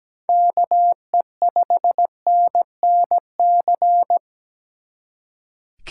Morse code, heard as a keyed tone, about 20 wpm, 700 Hz. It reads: KE5NNC
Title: Speed Racing - USA 2x3 Call Signs 17wpm